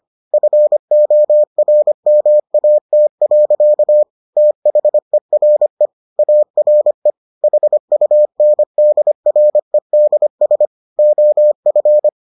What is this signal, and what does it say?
Morse code, heard as a keyed tone, about 25 wpm, 600 Hz. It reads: FORMAT. THERE ARE HUNDREDS OF